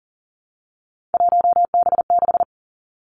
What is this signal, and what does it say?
Morse code, heard as a keyed tone, about 40 wpm, 690 Hz. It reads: 1B6
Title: Three Challenging Letter-Number Combinations 40wpm